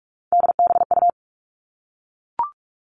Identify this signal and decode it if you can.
Morse code, heard as a keyed tone, about 44 wpm, 700 Hz. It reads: DBU
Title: All Cut Numbers - 3-Digit 45wpm